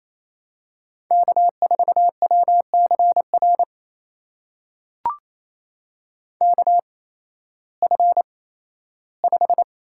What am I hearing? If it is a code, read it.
Morse code, heard as a keyed tone, about 28 wpm, spaced as for 15 wpm, 700 Hz. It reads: K4WCR KF5
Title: USA Call Signs - 5 Characters - 8x Character Spacing - 28wpm